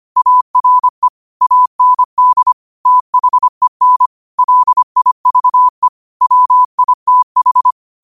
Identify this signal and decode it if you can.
Morse code, heard as a keyed tone, about 25 wpm, 1000 Hz. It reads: ARE AND THEN LIVE WITH
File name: qotdfiles / 25wpm